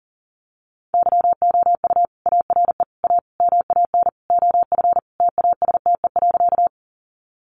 Morse code, heard as a keyed tone, about 40 wpm, 700 Hz. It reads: YOU ARE A MAN OF TASTE.
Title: Sentences from Top 1100 Words 40wpm